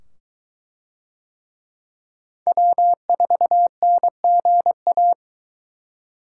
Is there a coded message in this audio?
Yes: Morse code from 2.5 to 5.1 s, heard as a keyed tone, about 20 wpm, 700 Hz. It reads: W4NGA